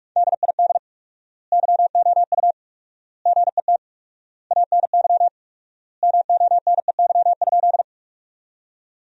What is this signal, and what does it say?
Morse code, heard as a keyed tone, about 46 wpm, 700 Hz. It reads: DID YOU GET ANY MONEY?
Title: 4x Word-Spacing - Sentences from Top 400 Words - 45wpm